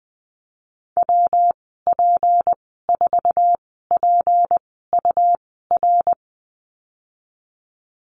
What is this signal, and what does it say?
Morse code, heard as a keyed tone, about 20 wpm, spaced as for 15 wpm, 700 Hz. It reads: WP4PUR
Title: 154 Lesson 25g - Letter p - Call Signs 20wpm